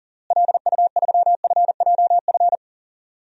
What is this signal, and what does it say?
Morse code, heard as a keyed tone, about 40 wpm, 700 Hz. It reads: LU3FJF
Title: DXCC-100 Call Signs - 6 Characters - Repeated 2x - 40wpm